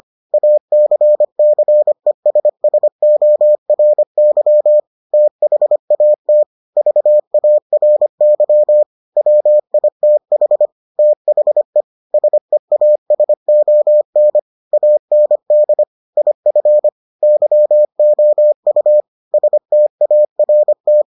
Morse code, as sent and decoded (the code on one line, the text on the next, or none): .- -.-. -.-. ... ... --- .-. -.-- / - .... .- - / ...- .- .-. -.-- / .-- .. - .... / - .... / ... .- ... --- -. / .- -. -.. / .. ..-. / -.-- --- ..- / ... - .- .-. -
ACCESSORY THAT VARY WITH THE SEASON AND IF YOU START